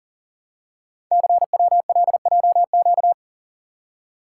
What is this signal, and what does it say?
Morse code, heard as a keyed tone, about 40 wpm, 700 Hz. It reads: CWLJQ
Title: Code Groups - 5 Letters - 40wpm